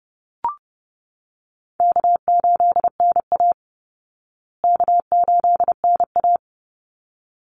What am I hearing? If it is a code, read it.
Morse code, heard as a keyed tone, about 30 wpm, 700 Hz. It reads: K8NA K8NA
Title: Easy Call Signs Repeated 2x - 30wpm